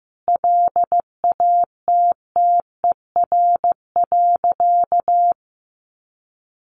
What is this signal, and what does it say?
Morse code, heard as a keyed tone, about 15 wpm, 700 Hz. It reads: LATTER.
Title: A Sound Of Thunder 15wpm